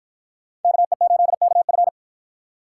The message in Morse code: KE8KF

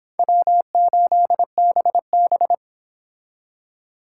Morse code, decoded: W8BB

